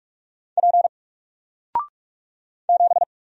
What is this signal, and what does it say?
Morse code, heard as a keyed tone, about 45 wpm, 700 Hz. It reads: P 7
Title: Single Letter-Number Alt 45wpm